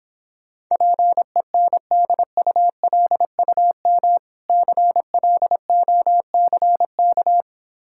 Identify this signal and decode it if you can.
Morse code, heard as a keyed tone, about 26 wpm, 700 Hz. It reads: PENDULUM CLOCK